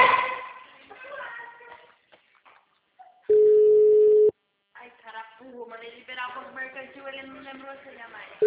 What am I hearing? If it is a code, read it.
Ringback tone (Europe, 425 Hz), with speech.